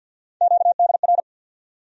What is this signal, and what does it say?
Morse code, heard as a keyed tone, about 53 wpm, 700 Hz. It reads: QDR
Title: USA Call Sign Postfixes - 3 Letters 50wpm